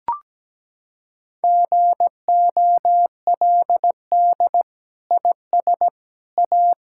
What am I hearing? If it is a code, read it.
Morse code, heard as a keyed tone, about 17 wpm, 700 Hz. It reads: GOLD IS A
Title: Sentences from Top 1000 Words - Review Part 6 - 17wpm